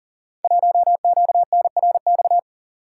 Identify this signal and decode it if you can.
Morse code, heard as a keyed tone, about 42 wpm, 700 Hz. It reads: 1QNRX
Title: Code Groups - 5 Letter:Numbers - 40wpm